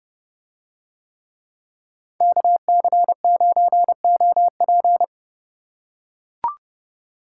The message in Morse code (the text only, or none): KC9OP